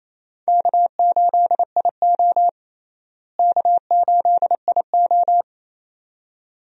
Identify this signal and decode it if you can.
Morse code, heard as a keyed tone, about 27 wpm, 700 Hz. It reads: K8IO K8IO
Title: USA Call Signs - 4 Characters - Repeated 2x - 28wpm